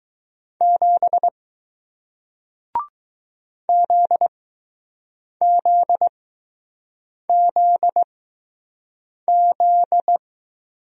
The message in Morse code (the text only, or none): 7 ZZZZ